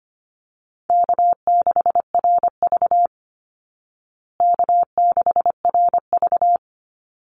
Morse code, heard as a keyed tone, about 25 wpm, 700 Hz. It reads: K6RV K6RV
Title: Easy Call Signs Repeated 2x - 25wpm